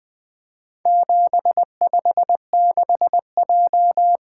Morse code, heard as a keyed tone, about 20 wpm, 700 Hz. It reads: 756J
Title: Four Letter-Number Combinations - 20wpm